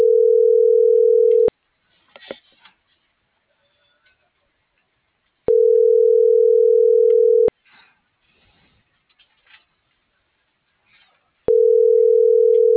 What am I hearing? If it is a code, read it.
Ringback tone (North America), 440 + 480 Hz.
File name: CENTRAL SKYBLUE / out axel.juarez